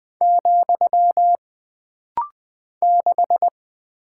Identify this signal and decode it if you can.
Morse code, heard as a keyed tone, about 20 wpm, 700 Hz. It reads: , 6
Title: Single Character (Letter:Number:Punctuation:Prosign) - Rapid-Fire - 20wpm